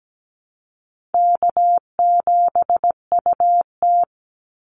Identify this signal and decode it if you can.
Morse code, heard as a keyed tone, about 17 wpm, 700 Hz. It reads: K7UT
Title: Easy Call Signs Repeated 2x - 17wpm